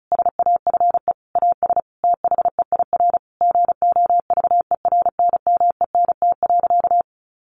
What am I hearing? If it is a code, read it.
Morse code, heard as a keyed tone, about 34 wpm, 700 Hz. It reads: SAFE AS THEIR GOVERNMENT.
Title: Sentences from Top 1500 Words 35wpm